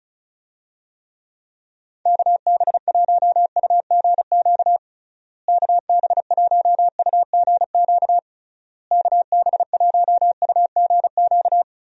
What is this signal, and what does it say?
Morse code, heard as a keyed tone, about 36 wpm, 700 Hz. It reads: KB1UGQ KB1UGQ KB1UGQ
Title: USA Call Signs - 6 Characters - Repeated 3x - 35wpm